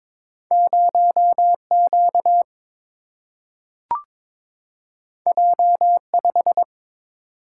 Morse code, heard as a keyed tone, about 21 wpm, 700 Hz. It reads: 0Q J5
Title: Single-Letter Single-Number 22wpm